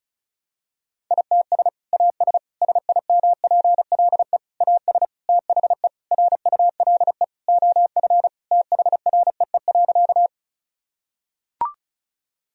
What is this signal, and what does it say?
Morse code, heard as a keyed tone, about 34 wpm, 700 Hz. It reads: ITS AS SIMPLE AS THE RULE OF THREE.
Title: Sentences from Top 1000 Words - Review Part 2 - 35wpm